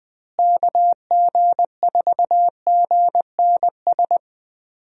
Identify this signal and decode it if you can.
Morse code, heard as a keyed tone, about 20 wpm, 700 Hz. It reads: KG4GNS
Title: USA Call Signs - 6 Characters - Repeated 2x - 20wpm